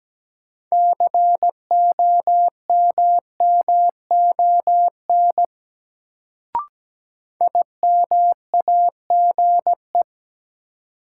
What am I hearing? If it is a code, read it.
Morse code, heard as a keyed tone, about 17 wpm, 700 Hz. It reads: COMMON IMAGE
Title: Sentences from Top 700 Words - New Word Drill - 17wpm